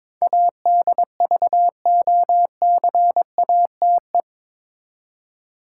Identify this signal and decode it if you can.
Morse code, heard as a keyed tone, about 22 wpm, 700 Hz. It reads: ADVOCATE
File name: practice / 22wpm / Eight Letter Words - 22wpm